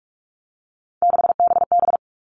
Morse code, heard as a keyed tone, about 44 wpm, 700 Hz. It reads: 6BB
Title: All Cut Numbers - 3-Digit 45wpm